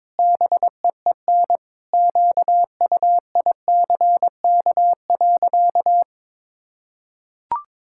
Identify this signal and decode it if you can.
Morse code, heard as a keyed tone, about 21 wpm, 700 Hz. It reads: BEEN QUICK.
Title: Sentences from Top 1000 Words - Review Part 5 - 22wpm